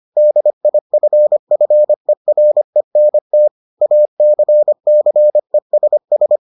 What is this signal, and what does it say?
Morse code, heard as a keyed tone, about 25 wpm, 600 Hz. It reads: DIFFERENT ACCESS